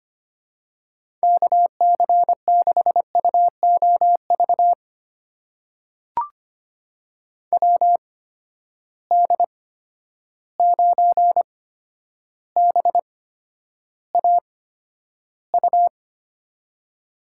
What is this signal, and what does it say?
Morse code, heard as a keyed tone, about 25 wpm, spaced as for 11 wpm, 700 Hz. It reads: KC6UOV WD9BAU
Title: USA Call Signs - 6 Characters - 8x Character Spacing - 25wpm